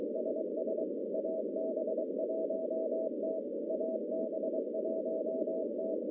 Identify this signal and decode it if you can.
Morse code, heard as a keyed tone, about 23 wpm, 640 Hz, with noise. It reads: SS AB1T AB1T
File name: contests / CW-SS-k5tr- -1100